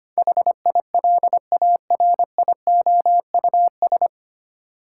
Morse code, heard as a keyed tone, about 25 wpm, 700 Hz. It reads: HILARIOUS
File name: Nine Letter Words - 25wpm